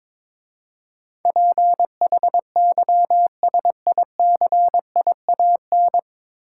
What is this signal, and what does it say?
Morse code, heard as a keyed tone, about 21 wpm, 700 Hz. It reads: PHYSICIAN
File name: 9 Letter Words - First 3 Letters Given - 22wpm